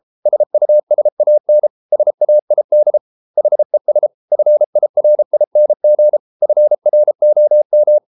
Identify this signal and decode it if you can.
Morse code, heard as a keyed tone, about 33 wpm, 600 Hz. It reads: SUSAN SAID HES FIRING FROM